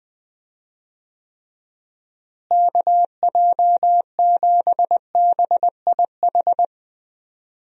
Morse code, heard as a keyed tone, about 20 wpm, 700 Hz. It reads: KJ7BIH
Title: USA Call Signs - 6 Characters - Repeated 3x - 20wpm